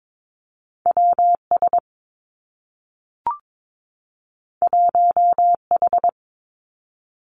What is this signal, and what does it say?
Morse code, heard as a keyed tone, about 22 wpm, 700 Hz. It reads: W S 1 H
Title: Two Challenging Letter-Number Combinations 22wpm